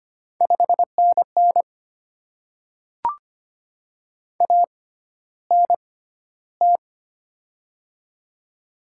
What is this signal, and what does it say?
Morse code, heard as a keyed tone, about 25 wpm, spaced as for 13 wpm, 700 Hz. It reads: 5NN ANT
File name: Top 20 QSO Elements - 6x Character Spacing - 25wpm